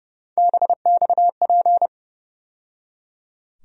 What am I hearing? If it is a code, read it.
Morse code, heard as a keyed tone, about 30 wpm, 700 Hz. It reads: BXP
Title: USA Call Sign Postfixes - 3 Letters 30wpm